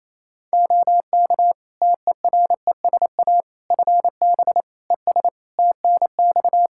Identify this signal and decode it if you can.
Morse code, heard as a keyed tone, about 28 wpm, 700 Hz. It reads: OK TERESA FB ES TNX
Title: Practice QSO - Exchange 2 - Info - 28wpm